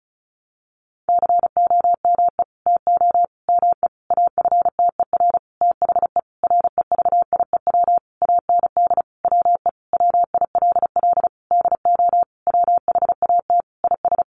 Morse code, heard as a keyed tone, about 34 wpm, 700 Hz. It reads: COME TO ME AFTER THE REVIEW AND WE WILL DO WHAT IS
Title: Sentences from Top 900 Words 35wpm